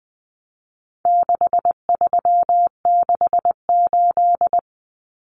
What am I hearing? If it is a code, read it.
Morse code, heard as a keyed tone, about 20 wpm, 700 Hz. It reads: 6368
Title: Common Cut Numbers - 4-Digit 20wpm